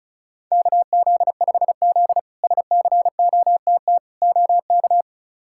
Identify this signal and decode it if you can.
Morse code, heard as a keyed tone, about 36 wpm, 700 Hz. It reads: KZ5Z SCOTT OK
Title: K1USN SST QSO Practice - 35wpm